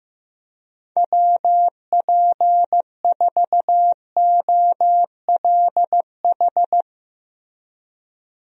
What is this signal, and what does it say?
Morse code, heard as a keyed tone, about 15 wpm, 700 Hz. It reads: WP4OLH
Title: USA Call Signs - 6 Characters - Repeated 3x - 15wpm